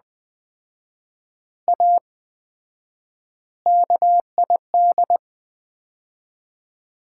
Morse code, heard as a keyed tone, about 21 wpm, 700 Hz. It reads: A KID
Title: 4x Word-Spacing - Sets of 5 Words - 20wpm